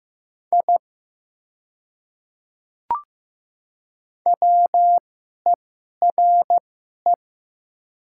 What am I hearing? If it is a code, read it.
Morse code, heard as a keyed tone, about 15 wpm, spaced as for 11 wpm, 700 Hz. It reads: I WERE
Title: 2x Character Spacing - Top 200 Words - 15wpm